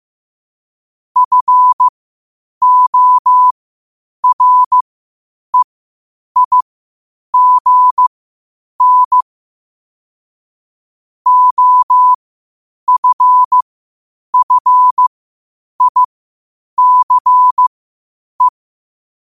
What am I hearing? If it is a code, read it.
Morse code, heard as a keyed tone, about 15 wpm, spaced as for 8 wpm, 1000 Hz. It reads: FOREIGN OFFICE
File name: qotdfiles / 8wpm